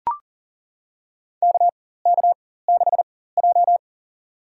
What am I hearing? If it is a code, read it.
Morse code, heard as a keyed tone, about 40 wpm, spaced as for 19 wpm, 700 Hz. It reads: KK6J
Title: USA Call Signs - 4 Characters - 4x Character Spacing - 40wpm